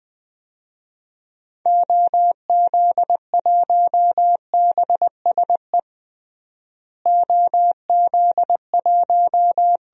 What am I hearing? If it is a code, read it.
Morse code, heard as a keyed tone, about 20 wpm, 700 Hz. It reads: OZ1BSE OZ1
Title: DXCC-100 Call Signs - 6 Characters - Repeated 3x - 20wpm